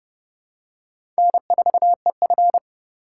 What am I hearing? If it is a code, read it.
Morse code, heard as a keyed tone, about 32 wpm, 700 Hz. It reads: N4EF
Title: USA Call Signs - 4x Character Spacing - 30wpm